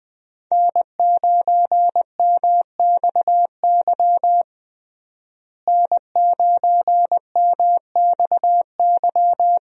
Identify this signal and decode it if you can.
Morse code, heard as a keyed tone, about 20 wpm, 700 Hz. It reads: N9MXY N9MXY
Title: USA Call Signs - 5 Characters - Repeated 3x - 20wpm